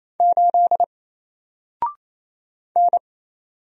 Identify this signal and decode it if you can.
Morse code, heard as a keyed tone, about 28 wpm, 700 Hz. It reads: 8 N
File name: Single-Letter-Number-28wpm